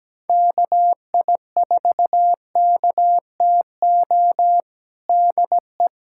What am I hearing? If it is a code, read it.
Morse code, heard as a keyed tone, about 17 wpm, 700 Hz. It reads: KI4KTO DE
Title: USA Callsign DE Callsign - 6 Characters - 17wpm